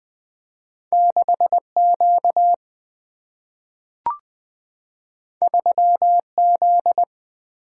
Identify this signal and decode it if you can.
Morse code, heard as a keyed tone, about 20 wpm, 700 Hz. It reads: 6 Q 3 Z
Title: Single-Letter Single-Number 20wpm